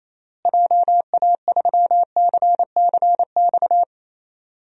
Morse code, heard as a keyed tone, about 28 wpm, 700 Hz. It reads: JA3CCX
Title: DXCC-100 Call Signs - 6 Characters - Repeated 3x - 28wpm